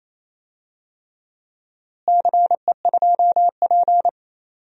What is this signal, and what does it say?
Morse code, heard as a keyed tone, about 28 wpm, 700 Hz. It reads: CE2P
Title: DXCC-100 Call Signs - 4 Characters - Repeated 3x - 28wpm